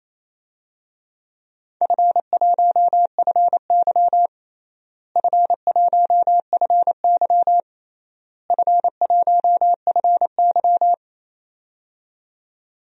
Morse code, heard as a keyed tone, about 27 wpm, 700 Hz. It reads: F1FY F1FY F1FY
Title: DXCC-100 Call Signs - 4 Characters - Repeated 3x - 28wpm